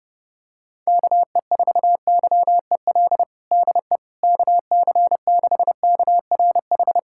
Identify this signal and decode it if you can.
Morse code, heard as a keyed tone, about 32 wpm, 700 Hz. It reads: KE4YEL DE KC6KRH
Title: USA Callsign DE Callsign 30wpm